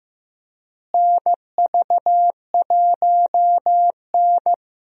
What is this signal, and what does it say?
Morse code, heard as a keyed tone, about 15 wpm, 700 Hz. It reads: NV1N
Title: USA Call Signs - 4 Characters - Repeated 3x - 15wpm